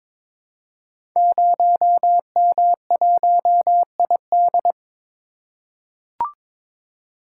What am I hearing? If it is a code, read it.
Morse code, heard as a keyed tone, about 23 wpm, 700 Hz. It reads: 0M1ID
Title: Code Groups - 5 Letter:Numbers - 22wpm